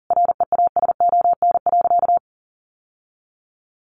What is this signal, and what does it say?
Morse code, heard as a keyed tone, about 40 wpm, 700 Hz. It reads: REASON.
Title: Sentences from Top 700 Words 40wpm